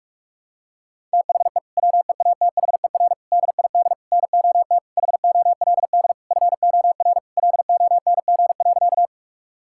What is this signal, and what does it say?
Morse code, heard as a keyed tone, about 41 wpm, 700 Hz. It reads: THE WEATHER DID NOT HOLD FOR LONG.
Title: Sentences from Top 1000 Words - Review Part 5 - 45wpm